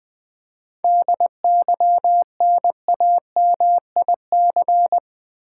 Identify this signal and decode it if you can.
Morse code, heard as a keyed tone, about 20 wpm, 700 Hz. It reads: DYNAMIC
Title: 7 Letter Words - First 3 Letters Given - 20wpm